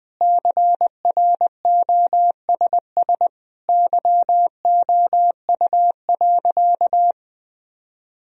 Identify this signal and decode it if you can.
Morse code, heard as a keyed tone, about 20 wpm, 700 Hz. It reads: CROSS YOU.
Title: Sentences from Top 1000 Words - Review Part 1 - 20wpm